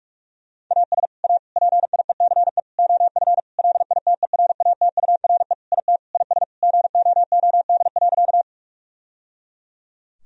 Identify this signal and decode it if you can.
Morse code, heard as a keyed tone, about 46 wpm, 700 Hz. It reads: AS A PIECE OF LITERATURE IT IS GOOD.
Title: Sentences from Top 1000 Words - Review Part 3 - 45wpm